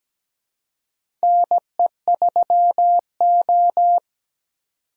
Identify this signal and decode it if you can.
Morse code, heard as a keyed tone, about 17 wpm, 700 Hz. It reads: NE3O
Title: USA Call Signs - 4 Characters - 3x Character Spacing - 17wpm